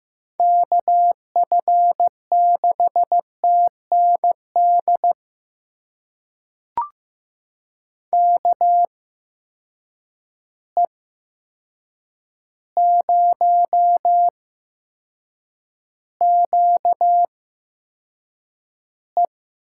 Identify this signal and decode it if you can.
Morse code, heard as a keyed tone, about 15 wpm, spaced as for 7 wpm, 700 Hz. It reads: KF6TND KE0QE
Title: USA Call Signs - 6 Characters - 8x Character Spacing - 15wpm